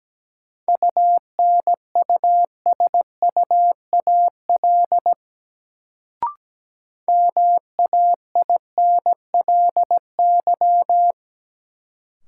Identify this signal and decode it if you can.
Morse code, heard as a keyed tone, about 17 wpm, 700 Hz. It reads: UNUSUAL MAINLY
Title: Sentences from Top 1400 Words - New Word Drill - 17wpm